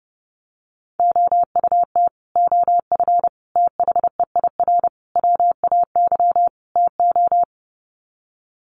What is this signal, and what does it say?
Morse code, heard as a keyed tone, about 30 wpm, 700 Hz. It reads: OUT OF THEIR WAY TO
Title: Sets of 5 Words Encore2 30wpm